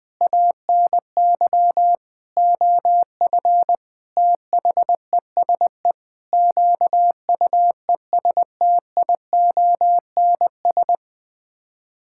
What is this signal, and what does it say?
Morse code, heard as a keyed tone, about 20 wpm, 700 Hz. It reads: ANY OF THESE QUESTIONS
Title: Sets of 9 Words - 20wpm